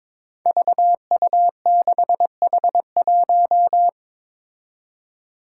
Morse code, heard as a keyed tone, about 22 wpm, 700 Hz. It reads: VU6H1
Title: Five Challenging Letter-Number Combinations 22wpm